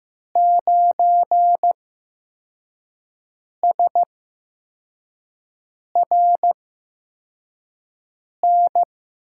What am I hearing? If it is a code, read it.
Morse code, heard as a keyed tone, about 16 wpm, spaced as for 4 wpm, 700 Hz. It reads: 9SRN
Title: USA Call Signs - 6 Characters - 8x Character Spacing - 15wpm